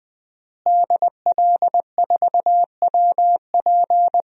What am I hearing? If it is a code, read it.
Morse code, heard as a keyed tone, about 20 wpm, 700 Hz. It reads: DL4WP
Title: DXCC-100 Call Signs - 5 Characters - Repeated 2x - 20wpm